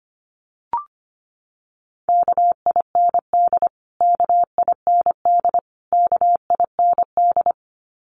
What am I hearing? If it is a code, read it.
Morse code, heard as a keyed tone, about 25 wpm, 700 Hz. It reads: KIND KIND KIND
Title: Top 200 Words Repeated 25wpm